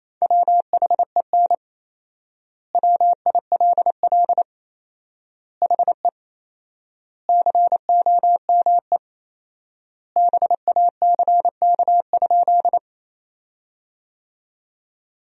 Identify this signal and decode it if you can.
Morse code, heard as a keyed tone, about 28 wpm, 700 Hz. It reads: WHEN WILL HE COME BACK?
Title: 4x Word-Spacing - Sentences from Top 200 Words - 28wpm